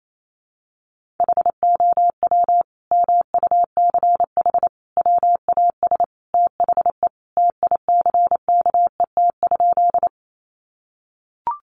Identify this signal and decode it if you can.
Morse code, heard as a keyed tone, about 28 wpm, 700 Hz. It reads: HOW MUCH WAS THE TICKET?
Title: Sentences from Top 1400 Words 28wpm